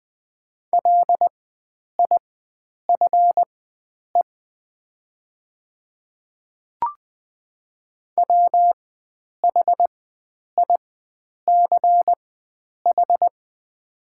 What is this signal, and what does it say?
Morse code, heard as a keyed tone, about 20 wpm, spaced as for 9 wpm, 700 Hz. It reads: LIFE WHICH
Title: 4x Character Spacing - Top 300 Words - 20wpm